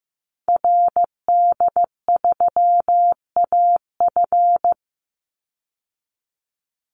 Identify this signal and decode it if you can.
Morse code, heard as a keyed tone, about 15 wpm, 700 Hz. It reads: RD3AF
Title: Call Signs Repeated 2x - 15wpm